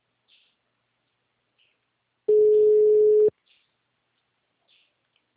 Ringback tone (Europe, 425 Hz).
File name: out- -2020- - - .40